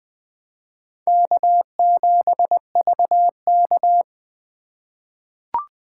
Morse code, heard as a keyed tone, about 20 wpm, 700 Hz. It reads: K7VK
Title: 2023 SOTA Chaser Call Signs - 20wpm